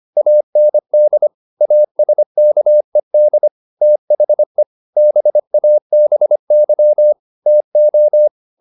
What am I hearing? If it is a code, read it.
Morse code, heard as a keyed tone, about 25 wpm, 600 Hz. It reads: AND ASKED THE BABY TO